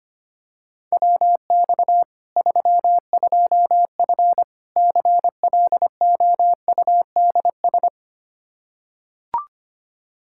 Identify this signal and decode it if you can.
Morse code, heard as a keyed tone, about 25 wpm, 700 Hz. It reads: WX 32F CLOUDS
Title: Practice QSO - Exchange 2 - Info - 25wpm